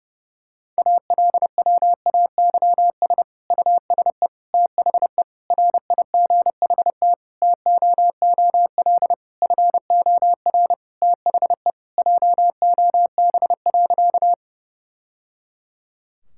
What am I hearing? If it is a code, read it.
Morse code, heard as a keyed tone, about 30 wpm, 700 Hz. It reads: ALWAYS USE THE RIGHT TOOL FOR THE JOB.
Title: Sentences from Top 1000 Words - Review Part 6 - 30wpm